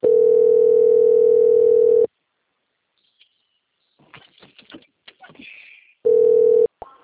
Ringback tone (North America), 440 + 480 Hz.